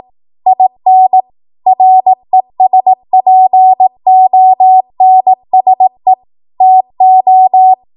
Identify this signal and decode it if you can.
Morse code, heard as a keyed tone, about 18 wpm, 750 Hz. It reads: IN RESPONSE TO
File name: Morse / 18WPM